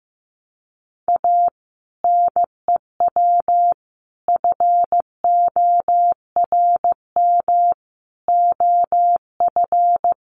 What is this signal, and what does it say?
Morse code, heard as a keyed tone, about 15 wpm, 700 Hz. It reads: A NEW FORM OF
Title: Sentences from Top 1300 Words 15wpm